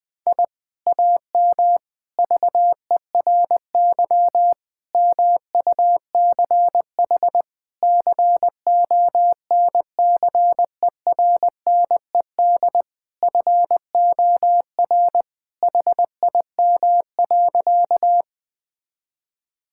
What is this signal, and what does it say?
Morse code, heard as a keyed tone, about 20 wpm, 700 Hz. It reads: I AM VERY MUCH CONCERNED FOR HIM.
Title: Sentences from Top 1000 Words - Review Part 6 - 20wpm